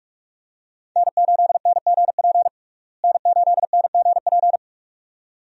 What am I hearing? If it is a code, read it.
Morse code, heard as a keyed tone, about 47 wpm, 700 Hz. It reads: N8NGP N8NGP
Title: USA Call Signs - 5 Characters - Repeated 2x - 45wpm